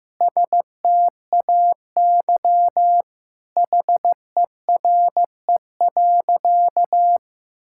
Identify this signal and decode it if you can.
Morse code, heard as a keyed tone, about 15 wpm, 700 Hz. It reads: STAY HERE.
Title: Sentences from Top 1600 Words 15wpm